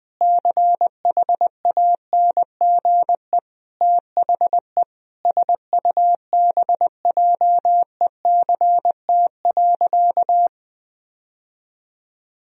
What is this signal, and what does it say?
Morse code, heard as a keyed tone, about 21 wpm, 700 Hz. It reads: CHANGE THE SUBJECT.
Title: Sentences from Top 1000 Words - Review Part 1 - 20wpm